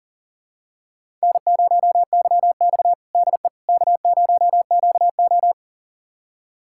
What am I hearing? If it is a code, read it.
Morse code, heard as a keyed tone, about 40 wpm, 700 Hz. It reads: N0YX DE K0QO
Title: USA Callsign DE Callsign - 4 Characters - 40wpm